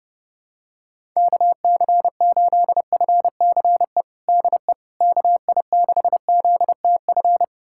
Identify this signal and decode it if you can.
Morse code, heard as a keyed tone, about 32 wpm, 700 Hz. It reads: KC8FCE DE KI6ZTF
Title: USA Callsign DE Callsign - 6 Characters - 30wpm